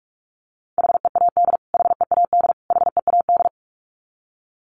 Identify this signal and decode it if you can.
Morse code, heard as a keyed tone, about 44 wpm, 700 Hz. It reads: HEAD HEAD HEAD
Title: Top 500 Words Repeated 45wpm